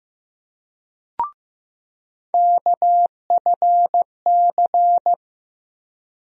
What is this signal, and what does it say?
Morse code, heard as a keyed tone, about 15 wpm, 700 Hz. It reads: KFC
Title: USA Call Sign Postfixes - 3 Letters 15wpm